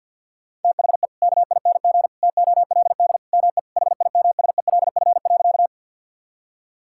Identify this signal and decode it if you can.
Morse code, heard as a keyed tone, about 50 wpm, 700 Hz. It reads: THE KING TOLD ME HIMSELF.
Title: Sentences from Top 1600 Words 50wpm